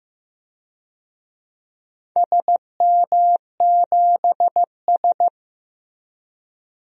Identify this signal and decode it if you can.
Morse code, heard as a keyed tone, about 16 wpm, 700 Hz. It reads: SM7S
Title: DXCC-100 Call Signs - 4 Characters - Repeated 2x - 15wpm